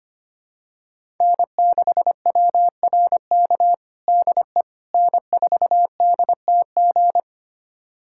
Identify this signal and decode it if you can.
Morse code, heard as a keyed tone, about 25 wpm, 700 Hz. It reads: N6WRK DE N4DTG
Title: USA Callsign DE Callsign 25wpm